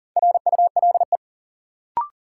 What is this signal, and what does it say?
Morse code, heard as a keyed tone, about 40 wpm, 700 Hz. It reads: RULE
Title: Sentences from Top 400 Words - New Word Drill - 40wpm